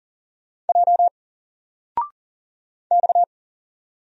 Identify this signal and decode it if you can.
Morse code, heard as a keyed tone, about 40 wpm, 700 Hz. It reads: J X
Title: Single Letter-Number 40wpm